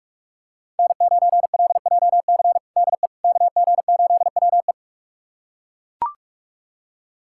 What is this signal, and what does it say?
Morse code, heard as a keyed tone, about 41 wpm, 700 Hz. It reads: N9LJC DE KG8WE
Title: USA Callsign DE Callsign - 5 Characters - 45wpm